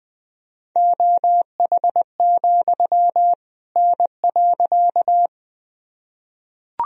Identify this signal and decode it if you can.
Morse code, heard as a keyed tone, about 21 wpm, 700 Hz. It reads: OH, N.
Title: Sentences from Top 500 Words and Abbreviations 20wpm